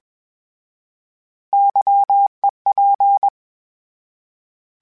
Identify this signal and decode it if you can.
Morse code, heard as a keyed tone, about 21 wpm, 800 Hz. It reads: YEP
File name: Misc / Misc-mp3s / Misc @20